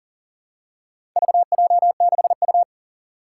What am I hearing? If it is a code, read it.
Morse code, heard as a keyed tone, about 38 wpm, 700 Hz. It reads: VJ6U
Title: Four Challenging Letter-Number Combinations 40wpm